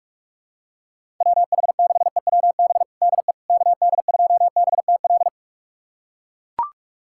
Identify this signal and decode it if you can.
Morse code, heard as a keyed tone, about 42 wpm, 700 Hz. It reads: WH6EWB DE KD2BTL
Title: USA Callsign DE Callsign - 6 Characters - 45wpm